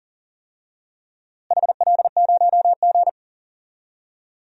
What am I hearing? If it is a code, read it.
Morse code, heard as a keyed tone, about 37 wpm, 700 Hz. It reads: HL0G
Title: DXCC-100 Call Signs - 4 Characters - Repeated 3x - 40wpm